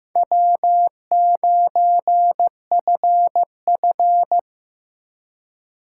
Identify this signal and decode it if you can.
Morse code, heard as a keyed tone, about 17 wpm, 700 Hz. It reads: W9FF
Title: Speed Racing - USA Call Signs 15wpm